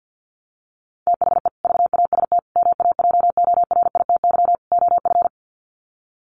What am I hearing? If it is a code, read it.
Morse code, heard as a keyed tone, about 50 wpm, 700 Hz. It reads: THE VAST MAJORITY OF
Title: Sets of 4 Words 50wpm